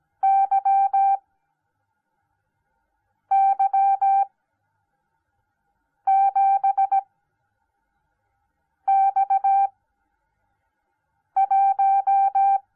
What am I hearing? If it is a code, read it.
Morse code, heard as a keyed tone, about 17 wpm, spaced as for 4 wpm, 780 Hz. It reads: YY7X1